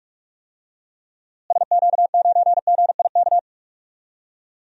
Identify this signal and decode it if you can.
Morse code, heard as a keyed tone, about 46 wpm, 700 Hz. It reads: SQ9GIK
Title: DXCC-100 Call Signs - 6 Characters - Repeated 3x - 45wpm